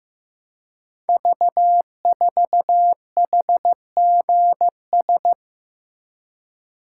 Morse code, heard as a keyed tone, about 15 wpm, 700 Hz. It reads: V4HGS
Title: Code Groups - 5 Letter:Numbers - 15wpm